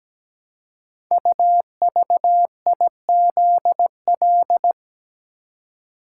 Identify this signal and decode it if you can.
Morse code, heard as a keyed tone, about 17 wpm, 700 Hz. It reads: UVIZL
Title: Code Groups - 5 Letters - 17wpm